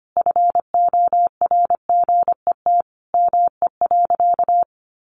Morse code, heard as a keyed tone, about 25 wpm, 700 Hz. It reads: FORGET ME.
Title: Sentences from Top 1200 Words 25wpm